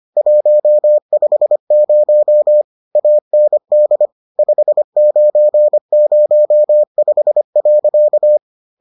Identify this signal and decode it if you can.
Morse code, heard as a keyed tone, about 25 wpm, 600 Hz. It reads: 150 AND 5905.